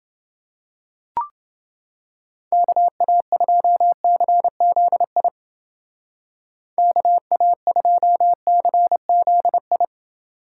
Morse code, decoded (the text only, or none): KA2CZI KA2CZI